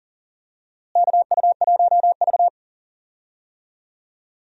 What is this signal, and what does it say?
Morse code, heard as a keyed tone, about 40 wpm, 700 Hz. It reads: KU1V
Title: USA Call Signs - 4 Characters - Repeated 3x - 40wpm